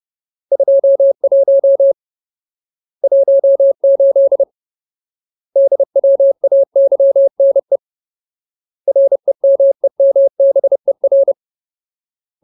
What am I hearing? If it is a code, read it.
Morse code, heard as a keyed tone, about 30 wpm, 550 Hz. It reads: 21 18 DWAYNE REMEMBER
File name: Ultimate Word List / 30WPM Ultimate